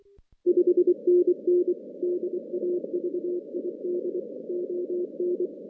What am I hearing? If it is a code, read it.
Morse code, heard as a keyed tone, about 23 wpm, 360 Hz. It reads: 5NN DAVID ON
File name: 2016 JAN CW NAQP / 2016 Jan naqp k5ot - 0-10